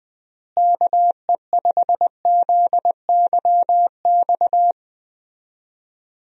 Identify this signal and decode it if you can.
Morse code, heard as a keyed tone, about 21 wpm, 700 Hz. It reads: KE5ZYX
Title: USA Call Signs - 6 Characters - Repeated 2x - 20wpm